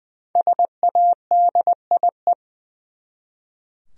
Morse code, heard as a keyed tone, about 20 wpm, 700 Hz. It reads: SADIE
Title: USA Common Names - 20wpm